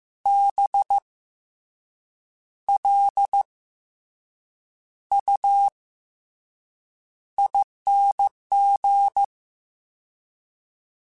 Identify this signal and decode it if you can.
Morse code, heard as a keyed tone, about 15 wpm, 800 Hz. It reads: B L U ING
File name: Special INGs / Special INGs-mp3s / SlowToFaster ing @13